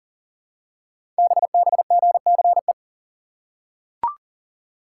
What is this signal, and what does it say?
Morse code, heard as a keyed tone, about 38 wpm, 700 Hz. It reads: BBGCE
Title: Code Groups - 5 Letter:Numbers - 40wpm